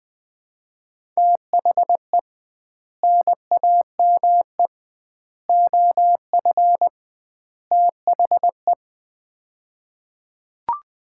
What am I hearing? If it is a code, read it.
Morse code, heard as a keyed tone, about 20 wpm, 700 Hz. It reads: THE NAME OF THE
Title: 2x Word-Spacing - Sets of 4 Words - 20wpm